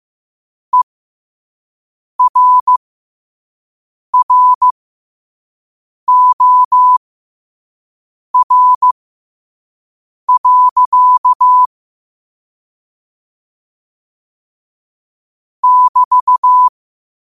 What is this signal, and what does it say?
Morse code, heard as a keyed tone, about 15 wpm, spaced as for 5 wpm, 1000 Hz. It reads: ERROR. <BT>
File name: qotdfiles / 5wpm